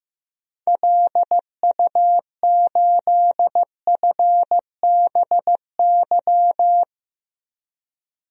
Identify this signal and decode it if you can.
Morse code, heard as a keyed tone, about 15 wpm, 690 Hz. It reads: LU8FBY
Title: DXCC-100 Call Signs - 6 Characters - 15wpm